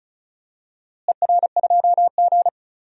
Morse code, heard as a keyed tone, about 34 wpm, 700 Hz. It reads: ER2G
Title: DXCC-100 Call Signs - 4 Characters - Repeated 3x - 35wpm